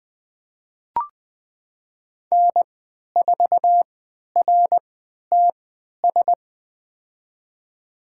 Morse code, heard as a keyed tone, about 20 wpm, spaced as for 11 wpm, 700 Hz. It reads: N4RTS
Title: USA Call Signs - 3x Character Spacing - 20wpm